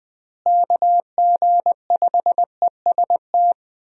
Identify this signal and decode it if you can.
Morse code, heard as a keyed tone, about 21 wpm, 700 Hz. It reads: KG5EST
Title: USA Call Signs - 6 Characters - Repeated 3x - 20wpm